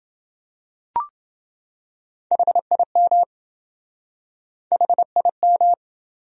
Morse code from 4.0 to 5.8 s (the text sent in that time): HIM